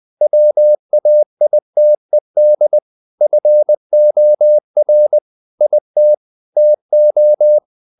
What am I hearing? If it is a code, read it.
Morse code, heard as a keyed tone, about 20 wpm, 600 Hz. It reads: WAITED FOR IT TO
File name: the old man and the sea / the old man and the sea part 1 20wpm